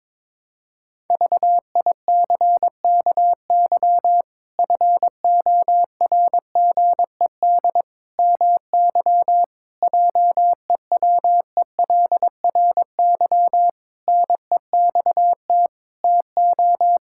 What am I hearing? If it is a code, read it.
Morse code, heard as a keyed tone, about 22 wpm, 700 Hz. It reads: VICKY FORGED MY JEWELRY NEXT TO
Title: Pangram Challenge v2 - 22wpm